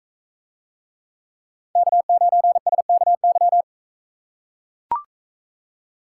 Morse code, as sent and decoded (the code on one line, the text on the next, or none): -.- ----. ... -.- -.--
K9SKY